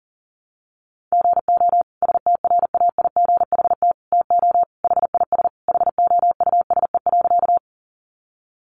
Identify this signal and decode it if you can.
Morse code, heard as a keyed tone, about 40 wpm, 700 Hz. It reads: GO STRAIGHT TO HIS HOUSE.
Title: Sentences from Top 1000 Words 40wpm